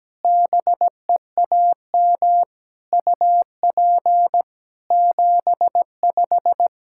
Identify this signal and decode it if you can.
Morse code, heard as a keyed tone, about 17 wpm, 700 Hz. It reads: BEAM UP 75
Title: Practice QSO - Exchange 2 - Info - 17wpm